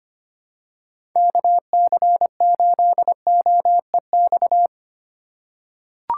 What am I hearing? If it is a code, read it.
Morse code, heard as a keyed tone, about 25 wpm, 690 Hz. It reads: KC8OEX
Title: USA Call Signs - 6 Characters - 3x Character Spacing - 25wpm